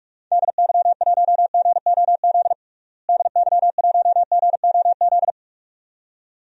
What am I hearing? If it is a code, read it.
Morse code, heard as a keyed tone, about 47 wpm, 700 Hz. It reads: DY1GOZ DY1GOZ
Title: DXCC-100 Call Signs - 6 Characters - Repeated 2x - 45wpm